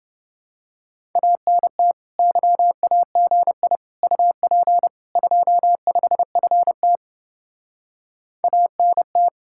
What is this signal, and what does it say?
Morse code, heard as a keyed tone, about 31 wpm, 700 Hz. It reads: ANT YAGI UP 25FT ANT